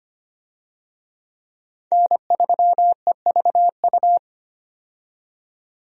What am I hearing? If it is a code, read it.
Morse code, heard as a keyed tone, about 25 wpm, 700 Hz. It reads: N3EVU
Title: USA Call Signs - 5 Characters - 25wpm